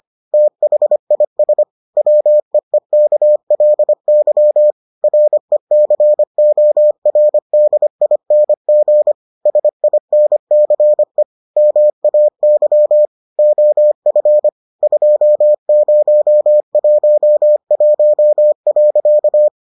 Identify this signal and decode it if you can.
Morse code, heard as a keyed tone, about 25 wpm, 600 Hz. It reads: THIS WEEKLY RECORDING SINCE MAY OF 2011.